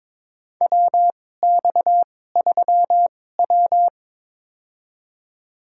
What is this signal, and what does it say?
Morse code, heard as a keyed tone, about 22 wpm, spaced as for 16 wpm, 700 Hz. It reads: WX3W
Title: USA Call Signs - 4 Characters - 2x Character Spacing - 22wpm